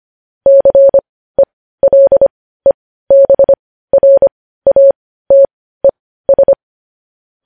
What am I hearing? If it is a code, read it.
Morse code, heard as a keyed tone, about 25 wpm, spaced as for 15 wpm, 550 Hz. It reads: CELEBRATES
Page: news-headlines-in-morse-code-at-15-wpm